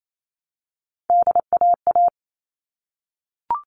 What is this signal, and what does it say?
Morse code, heard as a keyed tone, about 28 wpm, 700 Hz. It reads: DAA
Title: All Cut Numbers - 3-Digit 28wpm